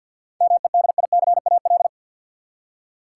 Morse code, heard as a keyed tone, about 47 wpm, 700 Hz. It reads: MEDICAL
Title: Seven Letter Words - 50wpm